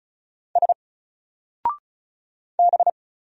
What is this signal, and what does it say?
Morse code, heard as a keyed tone, about 35 wpm, 700 Hz. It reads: S B
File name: Single-Letter-Number-35wpm